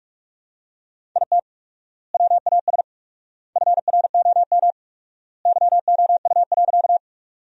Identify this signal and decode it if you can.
Morse code, heard as a keyed tone, about 44 wpm, 700 Hz. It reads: IT WAS FROM YOU.
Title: 4x Word-Spacing - Sentences from Top 100 Words - 45wpm